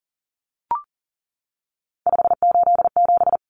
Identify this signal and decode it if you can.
Morse code, heard as a keyed tone, about 40 wpm, 700 Hz. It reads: 587
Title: Numbers - 3-Digit 40wpm